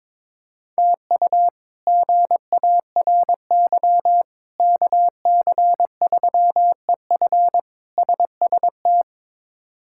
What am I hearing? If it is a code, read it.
Morse code, heard as a keyed tone, about 21 wpm, 700 Hz. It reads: TU GARY KC3EF SST
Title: K1USN SST QSO Practice - 22wpm